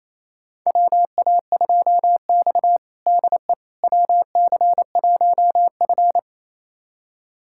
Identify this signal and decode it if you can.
Morse code, heard as a keyed tone, about 28 wpm, 700 Hz. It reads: WA2X DE WC1F
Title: USA Callsign DE Callsign - 4 Characters - 28wpm